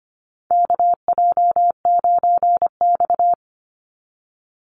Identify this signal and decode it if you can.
Morse code, heard as a keyed tone, about 25 wpm, 700 Hz. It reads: KJ9X
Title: NA SOTA Chaser Callsigns 25wpm